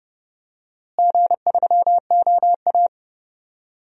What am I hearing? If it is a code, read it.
Morse code, heard as a keyed tone, about 31 wpm, 700 Hz. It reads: G3OA
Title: DXCC-100 Call Signs - 4 Characters - Repeated 3x - 30wpm